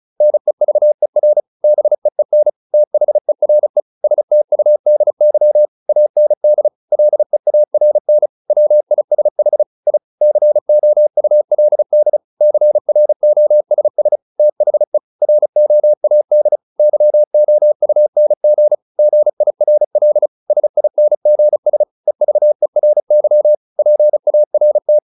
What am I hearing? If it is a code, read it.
Morse code, heard as a keyed tone, about 34 wpm, 600 Hz. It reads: NEVER BEEN THERE STUDY AND LEARN WISH I COULD CROSS THE ROAD YOUNG GIRL SINGS EVERY PART